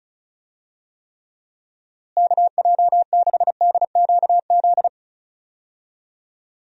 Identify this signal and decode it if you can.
Morse code, heard as a keyed tone, about 34 wpm, 700 Hz. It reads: KJ6DQZ
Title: USA Call Signs - 6 Characters - Repeated 2x - 35wpm